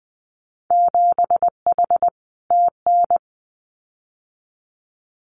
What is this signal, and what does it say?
Morse code, heard as a keyed tone, about 20 wpm, 700 Hz. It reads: 7H TN
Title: Winter Field Day Exchanges Encore 20wpm